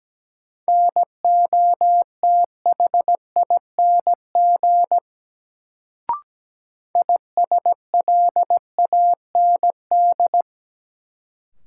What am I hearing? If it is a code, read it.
Morse code, heard as a keyed tone, about 17 wpm, 700 Hz. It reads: NOTHING ISLAND
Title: Sentences from Top 500 Words - New Word Drill - 17wpm